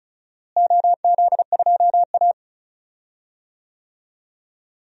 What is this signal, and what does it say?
Morse code, heard as a keyed tone, about 34 wpm, 700 Hz. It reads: OZ2A
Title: DXCC-100 Call Signs - 4 Characters - Repeated 3x - 35wpm